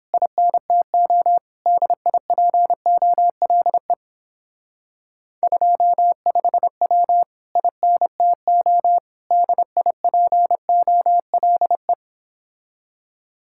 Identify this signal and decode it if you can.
Morse code, heard as a keyed tone, about 29 wpm, 700 Hz. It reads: INTO DIPOLE 25W INTO DIPOLE